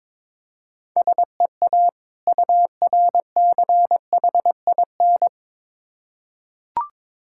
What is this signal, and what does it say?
Morse code, heard as a keyed tone, about 22 wpm, 700 Hz. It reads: SEA URCHIN
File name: Difficult Words with Hints - 22wpm